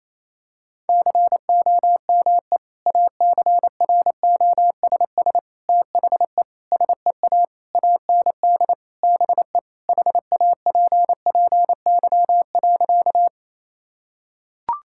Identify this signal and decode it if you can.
Morse code, heard as a keyed tone, about 28 wpm, 700 Hz. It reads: COME ACROSS THE SEA AND BE HAPPY.
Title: Sentences from Top 1000 Words - Review Part 5 - 28wpm